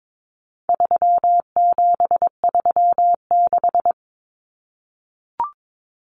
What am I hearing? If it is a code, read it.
Morse code, heard as a keyed tone, about 22 wpm, 700 Hz. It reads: 3736
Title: Numbers - 4-Digit 22wpm